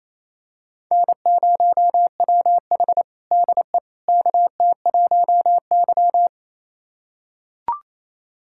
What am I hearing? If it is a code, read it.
Morse code, heard as a keyed tone, about 28 wpm, 700 Hz. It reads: N0WH DE KT1Y
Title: USA Callsign DE Callsign - 4 Characters - 28wpm